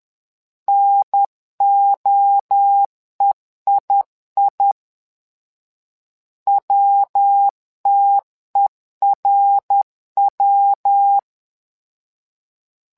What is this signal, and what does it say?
Morse code, heard as a keyed tone, about 10 wpm, 800 Hz. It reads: NOEII WTERW
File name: LEZIONI / 3 MWRO / 3 MWRO 10wpm 2148